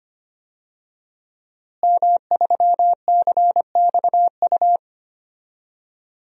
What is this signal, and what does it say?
Morse code, heard as a keyed tone, about 25 wpm, 700 Hz. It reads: M3CXU
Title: DXCC-100 Call Signs - 5 Characters - Repeated 3x - 25wpm